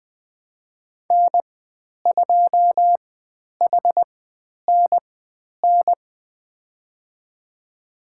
Morse code, heard as a keyed tone, about 20 wpm, spaced as for 10 wpm, 700 Hz. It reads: N2HNN
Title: 109 Lesson 18g - Letter m - Call Signs 20wpm